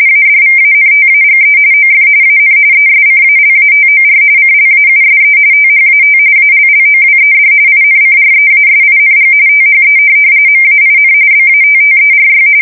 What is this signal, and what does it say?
RTTY, about 45.45 baud. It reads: ADVANTAGE OF AN ESCALATOR CLAUSE THAT ALLOWED INCREASING THE MAIN BATTERY
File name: fsk170 / USS-Massachusetts